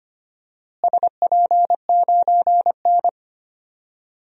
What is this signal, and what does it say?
Morse code, heard as a keyed tone, about 25 wpm, 700 Hz. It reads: SP9N
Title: DXCC-100 Call Signs - 4 Characters - Repeated 3x - 25wpm